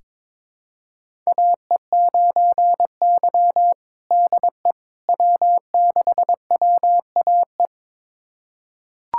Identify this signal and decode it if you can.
Morse code, heard as a keyed tone, about 22 wpm, 700 Hz. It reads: AE9Y DE W6WAE
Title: USA Callsign DE Callsign 22wpm